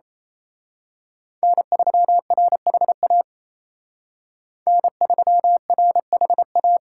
Morse code, heard as a keyed tone, about 30 wpm, 700 Hz. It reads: N3RHA N3RHA
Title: Speed Racing - USA 1x3 Call Signs 28wpm